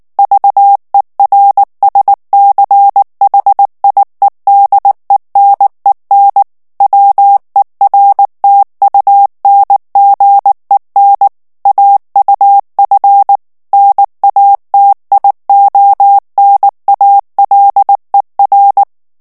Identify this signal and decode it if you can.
Morse code, heard as a keyed tone, about 19 wpm, 800 Hz. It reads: VERSCHIEDENEN WERTUNGEN AUF NATIONALER
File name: morse mp3 / ALLEMAND texte clair 2 vitesse 18 mots minute